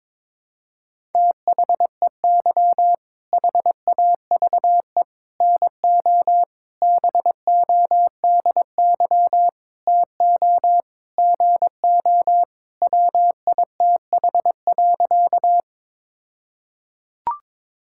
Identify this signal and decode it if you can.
Morse code, heard as a keyed tone, about 21 wpm, 700 Hz. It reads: THEY HAVE NO BODY TO GO WITH.
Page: morse-code-ninja-practice-22wpm